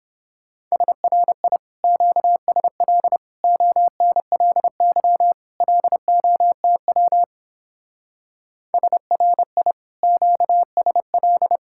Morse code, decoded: SRI QSL ONLY LOTW SRI QSL

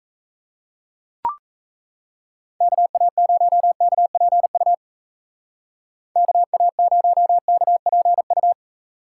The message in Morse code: KA0KPU KA0KPU

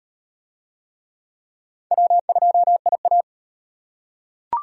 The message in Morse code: W2IA

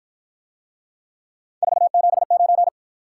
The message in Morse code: 479

